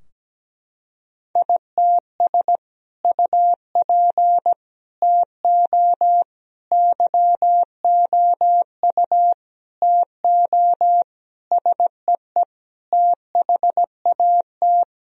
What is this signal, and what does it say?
Morse code, heard as a keyed tone, about 17 wpm, 700 Hz. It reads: ITS UP TO YOU TO SEE THAT
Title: Sentences from Top 1000 Words - Review Part 2 - 17wpm